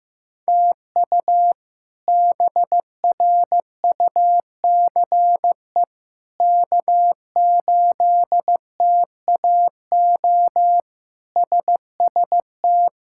Morse code, heard as a keyed tone, about 15 wpm, 700 Hz. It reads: TU BRUCE K8TAO SST
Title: K1USN SST QSO Practice - 15wpm